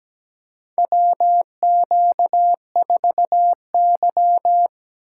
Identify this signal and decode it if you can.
Morse code, heard as a keyed tone, about 17 wpm, 700 Hz. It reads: WQ4Y
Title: USA Call Signs - 4 Characters - 2x Character Spacing - 17wpm